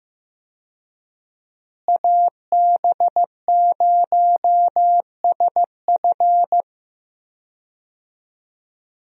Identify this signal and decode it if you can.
Morse code, heard as a keyed tone, about 15 wpm, 700 Hz. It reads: AB0SF
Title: USA Call Signs - 5 Characters - Repeated 3x - 15wpm